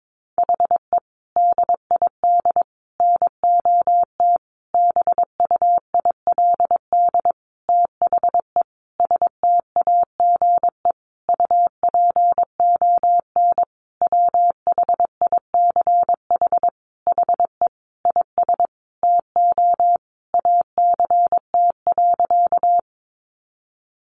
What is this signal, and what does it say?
Morse code, heard as a keyed tone, about 22 wpm, 700 Hz. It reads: HE DID NOT BUILD THE STAGE UPON WHICH HE IS TO ACT.
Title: Sentences from Top 1200 Words 22wpm